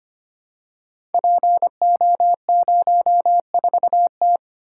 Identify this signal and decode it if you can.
Morse code, heard as a keyed tone, about 25 wpm, 700 Hz. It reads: PO04T
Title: Code Groups - 5 Letter:Numbers - 25wpm